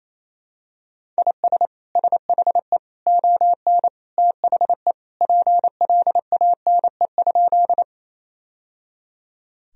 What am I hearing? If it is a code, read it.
Morse code, heard as a keyed tone, about 28 wpm, 700 Hz. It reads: IS SHE ON THE PLANE?
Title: Sentences from Top 1000 Words - Review Part 3 - 28wpm